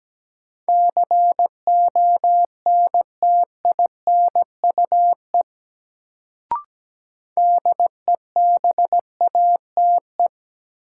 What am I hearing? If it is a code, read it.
Morse code, heard as a keyed tone, about 17 wpm, 700 Hz. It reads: CONTINUE DEBATE
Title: Sentences from Top 1100 Words - New Word Drill Alt - 17wpm